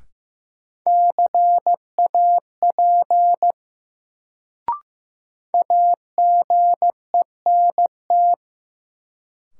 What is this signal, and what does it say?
Morse code, heard as a keyed tone, about 16 wpm, 700 Hz. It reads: CAP AGENT
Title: Sentences from Top 1600 Words - New Word Drill - 15wpm